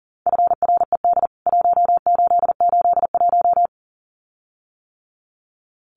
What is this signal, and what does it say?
Morse code, heard as a keyed tone, about 40 wpm, 700 Hz. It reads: FRED 1881
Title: CWT Exchange 40wpm